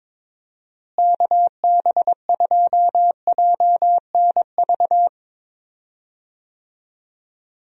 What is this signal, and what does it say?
Morse code, heard as a keyed tone, about 22 wpm, 700 Hz. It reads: KB2JNV
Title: USA Call Signs - 6 Characters - Repeated 3x - 22wpm